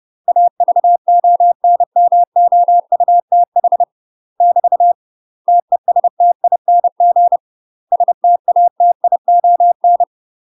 Morse code, heard as a keyed tone, about 30 wpm, 700 Hz. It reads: AVONMOUTH <BT> TESTING STATION
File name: UK MASTER QSOs TRAINING / 30WPM / UK MASTER QSOs 0011